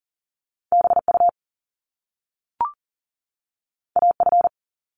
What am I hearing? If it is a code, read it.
Morse code, heard as a keyed tone, about 41 wpm, 700 Hz. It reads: BU AF
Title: Top Two Letter Combinations in Words 40wpm